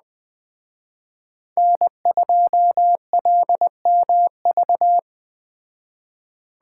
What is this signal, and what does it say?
Morse code, heard as a keyed tone, about 21 wpm, 700 Hz. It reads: N2LMV